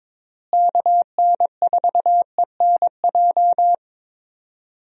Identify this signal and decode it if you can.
Morse code, heard as a keyed tone, about 22 wpm, 700 Hz. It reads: KN4ENJ
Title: USA Call Signs - 6 Characters - Repeated 2x - 22wpm